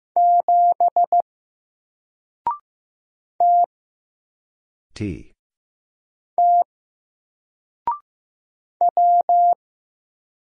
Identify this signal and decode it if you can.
Morse code, heard as a keyed tone, about 15 wpm, 700 Hz. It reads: 7 T T W
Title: Single Character (Letter:Number:Punctuation:Prosign) 15wpm